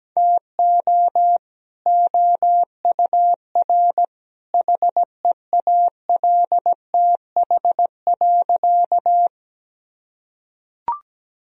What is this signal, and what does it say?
Morse code, heard as a keyed tone, about 17 wpm, 700 Hz. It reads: TO OUR HEALTH.
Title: Sentences from Top 1000 Words - Review Part 3 - 17wpm